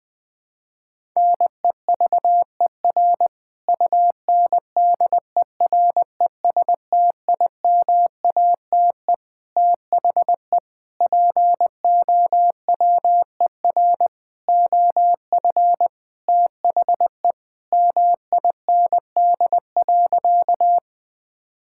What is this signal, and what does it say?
Morse code, heard as a keyed tone, about 21 wpm, 700 Hz. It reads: NEVER UNDERESTIMATE THE POWER OF THE MIND.
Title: Sentences with 4-Word Phrases - 20wpm